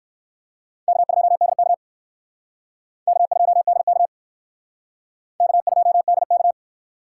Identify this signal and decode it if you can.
Morse code, heard as a keyed tone, about 59 wpm, 700 Hz. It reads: K2DK K2DK K2DK